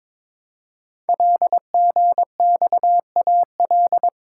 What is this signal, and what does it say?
Morse code, heard as a keyed tone, about 22 wpm, 700 Hz. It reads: LGXAL
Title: Code Groups - 5 Letters - 22wpm